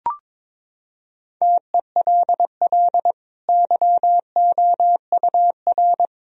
Morse code, heard as a keyed tone, about 22 wpm, 700 Hz. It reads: TELL YOUR
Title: Sentences from Top 1000 Words - Review Part 2 - 22wpm